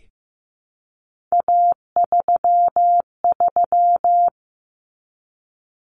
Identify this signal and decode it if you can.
Morse code, heard as a keyed tone, about 15 wpm, 700 Hz. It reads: A33
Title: Common Cut Numbers - 3-Digit 15wpm